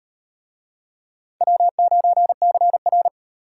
Morse code, heard as a keyed tone, about 38 wpm, 700 Hz. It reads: W9CR